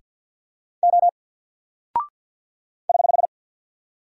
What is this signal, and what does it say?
Morse code, heard as a keyed tone, about 50 wpm, 700 Hz. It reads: O <HH>
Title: Single Character (Letter:Number:Punctuation:Prosign) 50wpm